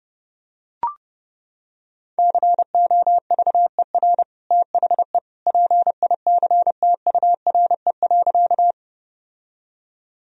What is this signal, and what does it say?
Morse code, heard as a keyed tone, about 30 wpm, 700 Hz. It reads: COVER THE PICTURE.
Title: Sentences from Top 1000 Words - Review Part 1 - 30wpm